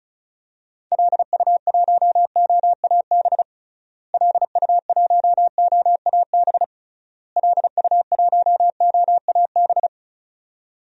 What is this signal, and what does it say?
Morse code, heard as a keyed tone, about 36 wpm, 700 Hz. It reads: LU1OAB LU1OAB LU1OAB
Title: DXCC-100 Call Signs - 6 Characters - Repeated 3x - 35wpm